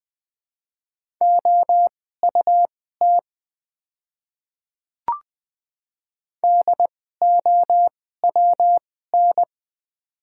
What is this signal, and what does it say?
Morse code, heard as a keyed tone, about 20 wpm, spaced as for 14 wpm, 700 Hz. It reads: OUT DOWN
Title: 2x Character Spacing - Top 100 Words - 20wpm